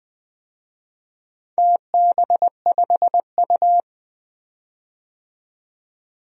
Morse code, heard as a keyed tone, about 20 wpm, 700 Hz. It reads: TB5U
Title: DXCC-100 Call Signs - 4 Characters - 20wpm